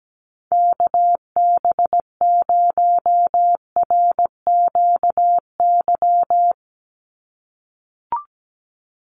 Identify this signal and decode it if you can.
Morse code, heard as a keyed tone, about 17 wpm, 700 Hz. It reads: KB0RQY
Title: USA Call Signs 17wpm